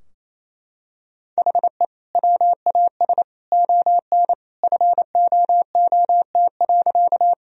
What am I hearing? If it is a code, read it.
Morse code, heard as a keyed tone, about 27 wpm, 700 Hz. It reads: HE WAS ON FOOT.
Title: Sentences from Top 1000 Words - Review Part 3 - 28wpm